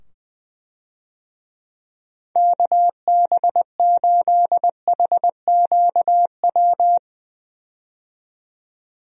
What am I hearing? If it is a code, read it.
Morse code, heard as a keyed tone, about 20 wpm, 700 Hz. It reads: KB8HQW
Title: USA Call Signs - 6 Characters - 20wpm